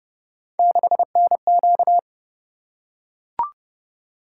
Morse code, heard as a keyed tone, about 29 wpm, 700 Hz. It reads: 6NQ